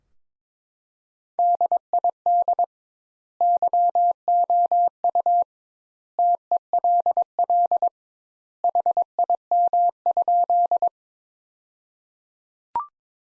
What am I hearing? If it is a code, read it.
Morse code, heard as a keyed tone, about 22 wpm, 700 Hz. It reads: DID YOU TELL HIM?
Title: Sentences Top 200 Words - 22wpm